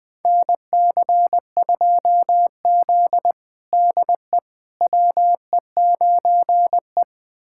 Morse code, heard as a keyed tone, about 21 wpm, 700 Hz. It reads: NC2Z DE WE9E
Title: USA Callsign DE Callsign - 4 Characters - 20wpm